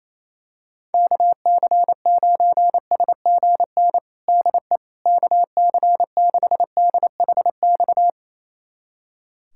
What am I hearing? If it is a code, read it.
Morse code, heard as a keyed tone, about 28 wpm, 700 Hz. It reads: KC9SGN DE KC6DHX
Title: USA Callsign DE Callsign 28wpm